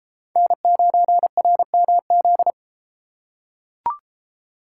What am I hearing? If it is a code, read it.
Morse code, heard as a keyed tone, about 32 wpm, 700 Hz. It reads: N9RMZ